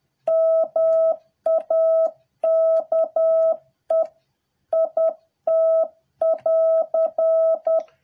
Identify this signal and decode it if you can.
Morse code, heard as a keyed tone, about 10 wpm, 650 Hz. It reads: MAKE IT<AR>